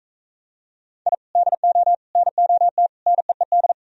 Morse code, heard as a keyed tone, about 45 wpm, 700 Hz. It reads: I DO NOT NEED